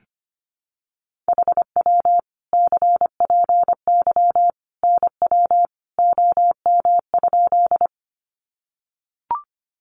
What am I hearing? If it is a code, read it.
Morse code, heard as a keyed tone, about 25 wpm, 700 Hz. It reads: HW CPY NW OM?
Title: Common QSO Phrases 25wpm